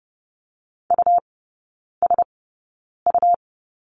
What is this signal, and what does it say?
Morse code, heard as a keyed tone, about 30 wpm, spaced as for 9 wpm, 700 Hz. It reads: USU
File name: Lesson 17e - Letter u - Three Characters 30wpm